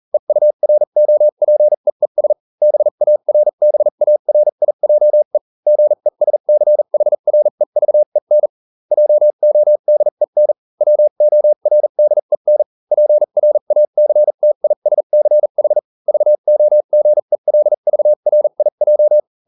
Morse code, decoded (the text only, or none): EUROPEES BARBARIJE GESCHREVEN JODEN WORDEN PRACTISCH VOGELVRIJ